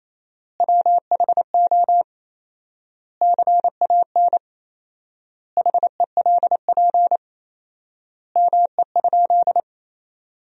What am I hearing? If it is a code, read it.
Morse code, heard as a keyed tone, about 28 wpm, 700 Hz. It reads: WHO CAN HELP ME?
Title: 4x Word-Spacing - Sentences from Top 200 Words - 28wpm